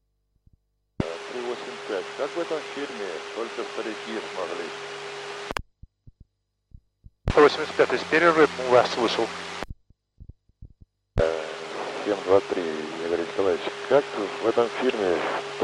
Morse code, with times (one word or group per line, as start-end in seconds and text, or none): none